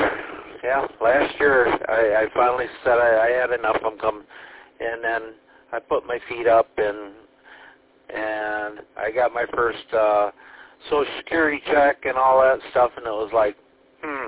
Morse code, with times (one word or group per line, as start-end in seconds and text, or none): none